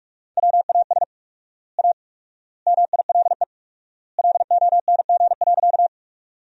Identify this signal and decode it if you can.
Morse code, heard as a keyed tone, about 44 wpm, 700 Hz. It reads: WAS A MILE LONG.
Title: 4x Word-Spacing - Sentences from Top 500 Words - 45wpm